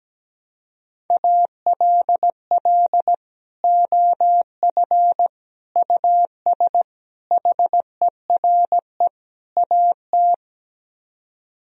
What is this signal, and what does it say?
Morse code, heard as a keyed tone, about 17 wpm, 700 Hz. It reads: ALL OF US HERE AT
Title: Speed-Racing - Sets of 5 Words 17wpm